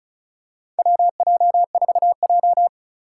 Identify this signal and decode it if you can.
Morse code, heard as a keyed tone, about 34 wpm, 700 Hz. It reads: WJ4J
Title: USA Call Signs - 2x Character Spacing - 35wpm